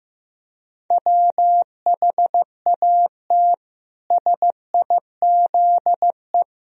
Morse code, heard as a keyed tone, about 15 wpm, 700 Hz. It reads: WHAT SIZE
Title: Sentences from Top 1000 Words - Review Part 3 - 15wpm